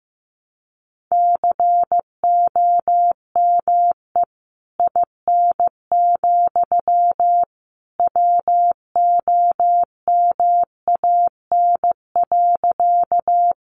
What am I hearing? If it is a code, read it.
Morse code, heard as a keyed tone, about 15 wpm, 700 Hz. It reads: COME IN, WOMAN.
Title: Sentences from Top 1100 Words 15wpm